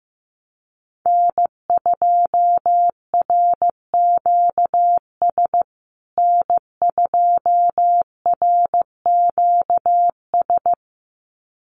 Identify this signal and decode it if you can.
Morse code, heard as a keyed tone, about 15 wpm, 700 Hz. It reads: N2RQS N2RQS
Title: USA Call Signs Repeated 2x 15wpm